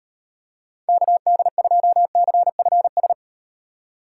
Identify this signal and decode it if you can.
Morse code, heard as a keyed tone, about 38 wpm, 700 Hz. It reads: KD2CFS